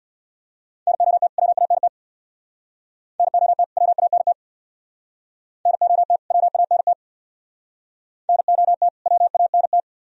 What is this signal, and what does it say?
Morse code, heard as a keyed tone, about 64 wpm, 700 Hz. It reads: NOT WANT NOT WANT NOT WANT NOT WANT